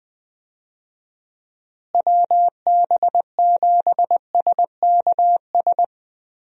Morse code, heard as a keyed tone, about 21 wpm, 700 Hz. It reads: WB7SKS
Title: USA Call Signs - 6 Characters - Repeated 3x - 20wpm